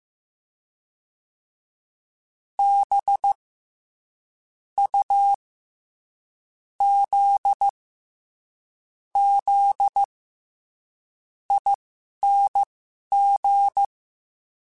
Morse code, heard as a keyed tone, about 15 wpm, spaced as for 11 wpm, 800 Hz. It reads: B U Z Z ING
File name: Special INGs-mp3s / Special INGs-mp3s / SlowToFaster ing @10